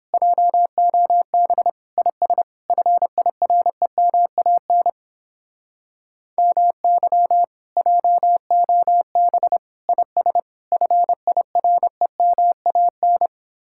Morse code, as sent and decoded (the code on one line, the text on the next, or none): .--- --- -... / .. ... / ..-. .. .-. -- .- -. / -- -.-- / .--- --- -... / .. ... / ..-. .. .-. -- .- -.
JOB IS FIREMAN MY JOB IS FIREMAN